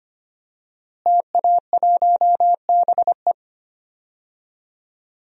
Morse code, heard as a keyed tone, about 25 wpm, 700 Hz. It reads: TA1BE
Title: DXCC-100 Call Signs - 5 Characters - Repeated 3x - 25wpm